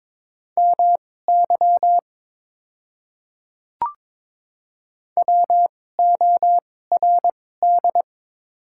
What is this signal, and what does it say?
Morse code, heard as a keyed tone, about 21 wpm, spaced as for 16 wpm, 700 Hz. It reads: MY WORD
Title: 2x Character Spacing - Top 100 Words - 22wpm